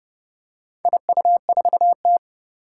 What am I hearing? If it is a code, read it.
Morse code, heard as a keyed tone, about 28 wpm, 700 Hz. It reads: IU4T
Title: Four Challenging Letter-Number Combinations 30wpm